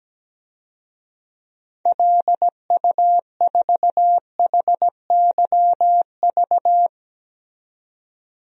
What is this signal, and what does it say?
Morse code, heard as a keyed tone, about 17 wpm, 700 Hz. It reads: LU4HYV
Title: DXCC-100 Call Signs - 6 Characters - 17wpm